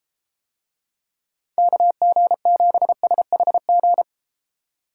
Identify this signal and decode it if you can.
Morse code, heard as a keyed tone, about 32 wpm, 700 Hz. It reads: KG7SHG